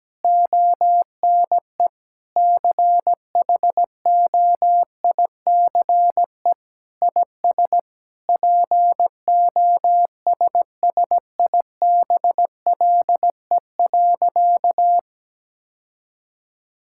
Morse code, heard as a keyed tone, about 17 wpm, 700 Hz. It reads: ONE CHOICE IS POSSIBLE.
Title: Sentences from Top 1000 Words - Review Part 6 - 17wpm